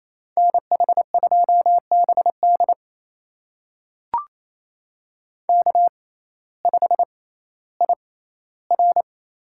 Morse code, heard as a keyed tone, about 28 wpm, spaced as for 15 wpm, 700 Hz. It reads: NH2BD K5IR
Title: USA Call Signs - 6x Character Spacing - 28wpm